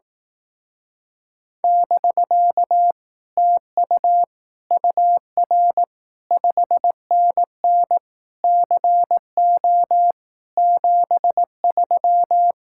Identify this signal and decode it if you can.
Morse code, heard as a keyed tone, about 18 wpm, 700 Hz. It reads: <BK> TU UR 5NN CO 73
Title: Speed-Racing Common QSO Phrases Encore - 15wpm